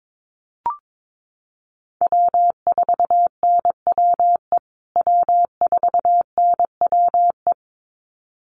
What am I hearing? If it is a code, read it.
Morse code, heard as a keyed tone, about 22 wpm, 700 Hz. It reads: W4NWE W4NWE
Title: USA Call Signs Repeated 2x 22wpm